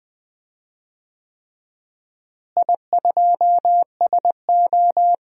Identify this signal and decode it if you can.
Morse code, heard as a keyed tone, about 20 wpm, 700 Hz. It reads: I2SO
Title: DXCC-100 Call Signs - 4 Characters - Repeated 3x - 20wpm